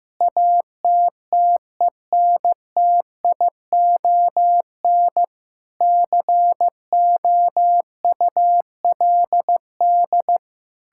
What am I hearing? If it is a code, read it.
Morse code, heard as a keyed tone, about 15 wpm, 700 Hz. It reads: ATTENTION COULD
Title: Sentences from Top 1000 Words - Review Part 6 - 15wpm